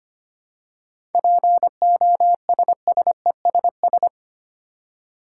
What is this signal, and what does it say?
Morse code, heard as a keyed tone, about 25 wpm, 700 Hz. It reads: POSSESS
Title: Seven Letter Words - 25wpm